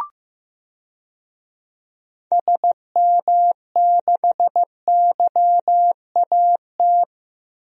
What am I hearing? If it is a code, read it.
Morse code, heard as a keyed tone, about 15 wpm, 700 Hz. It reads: SM6YAT
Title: DXCC-100 Call Signs - 6 Characters - Repeated 2x - 15wpm